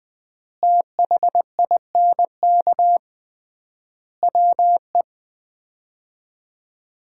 Morse code, heard as a keyed tone, about 20 wpm, 700 Hz. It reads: THINK WE
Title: 3x Word-Spacing - Sets of 5 Words - 20wpm